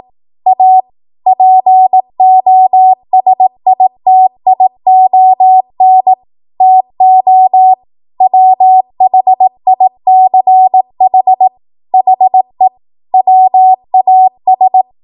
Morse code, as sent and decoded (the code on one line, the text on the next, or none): .- / .--. --- ... .. - .. --- -. / - --- / .-- .... .. -.-. .... / .... / .-- .- ...
A POSITION TO WHICH HE WAS